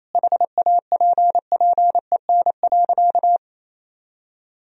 Morse code, heard as a keyed tone, about 28 wpm, 700 Hz. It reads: HAPPEN.
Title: Sentences from Top 1000 Words - Review Part 4 - 28wpm